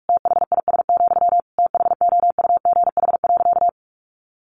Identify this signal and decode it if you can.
Morse code, heard as a keyed tone, about 44 wpm, 700 Hz. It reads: THIS, THOUGH.
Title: Sentences from Top 1400 Words 45wpm